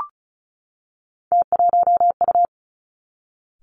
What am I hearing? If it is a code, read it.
Morse code, heard as a keyed tone, about 36 wpm, 700 Hz. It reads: T1U
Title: Three Challenging Letter-Number Combinations 35wpm